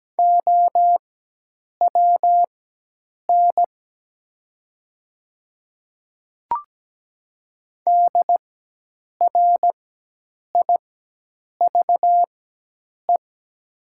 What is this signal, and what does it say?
Morse code, heard as a keyed tone, about 17 wpm, spaced as for 8 wpm, 700 Hz. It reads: OWN DRIVE
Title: 4x Character Spacing - Top 500 Words - 17wpm